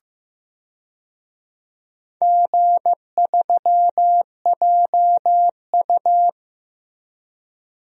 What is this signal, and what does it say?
Morse code, heard as a keyed tone, about 15 wpm, 700 Hz. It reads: G3JU
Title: DXCC-100 Call Signs - 4 Characters - Repeated 2x - 15wpm